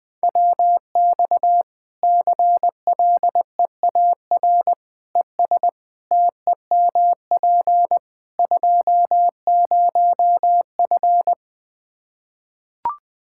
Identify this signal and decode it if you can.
Morse code, heard as a keyed tone, about 21 wpm, 700 Hz. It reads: WX CLEAR ES TEMP 20F
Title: K1USN SST QSO Practice 20wpm